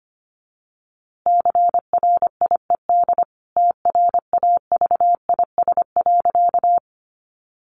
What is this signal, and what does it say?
Morse code, heard as a keyed tone, about 25 wpm, 700 Hz. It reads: CRIED TRAVIS.
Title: A Sound Of Thunder 25wpm